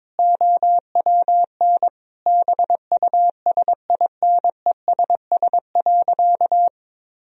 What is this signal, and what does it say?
Morse code, heard as a keyed tone, about 22 wpm, 700 Hz. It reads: OWN BUSINESS.
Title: Sentences from Top 1000 Words - Review Part 2 - 22wpm